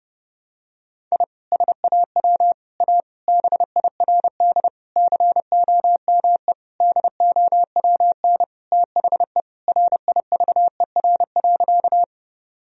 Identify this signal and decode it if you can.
Morse code, heard as a keyed tone, about 30 wpm, 700 Hz. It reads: I SAW A BIRD COME DOWN THE RIVER.
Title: Sentences from Top 1000 Words - Review Part 2 - 30wpm